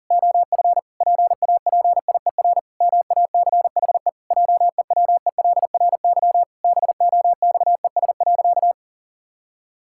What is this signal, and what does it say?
Morse code, heard as a keyed tone, about 37 wpm, 700 Hz. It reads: OF PAPIER MACHE JEWELRY BOXES.
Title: Pangram Challenge v2 - 40wpm